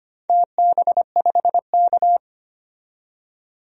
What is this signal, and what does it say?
Morse code, heard as a keyed tone, about 25 wpm, 700 Hz. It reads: TB5K
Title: DXCC-100 Call Signs - 4 Characters - 25wpm